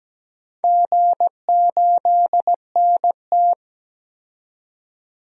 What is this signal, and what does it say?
Morse code, heard as a keyed tone, about 17 wpm, 700 Hz. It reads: G8NT
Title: DXCC-100 Call Signs - 4 Characters - 17wpm